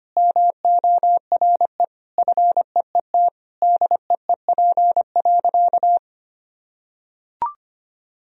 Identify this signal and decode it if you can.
Morse code, heard as a keyed tone, about 25 wpm, 700 Hz. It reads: MORE FEET DEEP.
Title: Sentences from Top 1000 Words - Review Part 3 - 25wpm